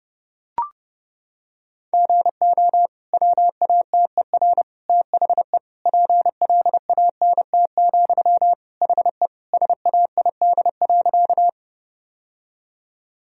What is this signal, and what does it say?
Morse code, heard as a keyed tone, about 30 wpm, 700 Hz. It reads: GO WATER THE PLANT, HE SAID.
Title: Sentences from Top 1000 Words - Review Part 1 - 30wpm